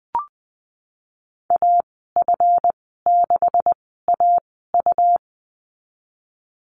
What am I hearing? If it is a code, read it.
Morse code, heard as a keyed tone, about 20 wpm, spaced as for 15 wpm, 700 Hz. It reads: AF6AU
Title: 186 Lesson 30g - Letter b - Call Signs 20wpm